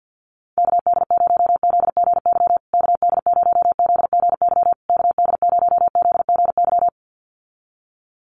Morse code, heard as a keyed tone, about 50 wpm, 700 Hz. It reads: KD0ZGY KD0ZGY KD0ZGY
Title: USA Call Signs Repeated 3x 50wpm